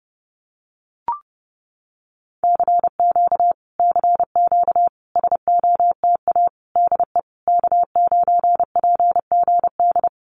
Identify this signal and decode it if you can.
Morse code, heard as a keyed tone, about 32 wpm, 700 Hz. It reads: CQ CQ SOTA DE K9PGD
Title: Common QSO Phrases Encore 30wpm